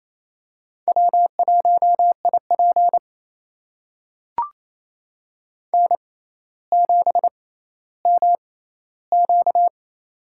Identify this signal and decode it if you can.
Morse code, heard as a keyed tone, about 28 wpm, spaced as for 14 wpm, 700 Hz. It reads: W1IP N7MQ
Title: USA Call Signs - 4 Characters - 6x Character Spacing - 28wpm